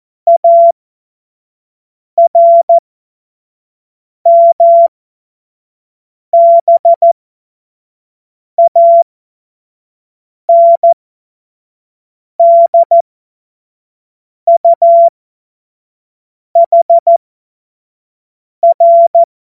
Morse code, heard as a keyed tone, about 14 wpm, spaced as for 5 wpm, 680 Hz. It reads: ARMBANDUHR
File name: Morsekurs / Witze / Witz 19 Dezember 2023 25 F70